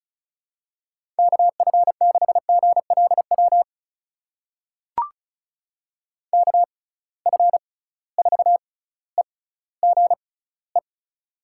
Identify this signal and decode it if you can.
Morse code, heard as a keyed tone, about 36 wpm, spaced as for 18 wpm, 700 Hz. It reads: KF6GLW KF4EGE
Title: USA Call Signs - 6 Characters - 6x Character Spacing - 35wpm